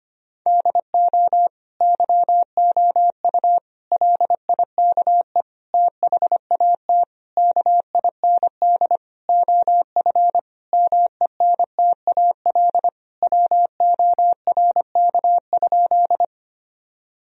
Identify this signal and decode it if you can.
Morse code, heard as a keyed tone, about 25 wpm, 700 Hz. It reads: DO YOU LIKE THAT KIND OF MENTAL WORK?
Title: Sentences from Top 1000 Words - Review Part 4 - 25wpm